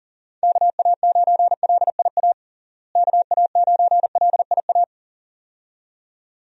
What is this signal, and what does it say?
Morse code, heard as a keyed tone, about 40 wpm, 700 Hz. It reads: KA9LIA KA9LIA
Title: USA Call Signs - 6 Characters - Repeated 2x - 40wpm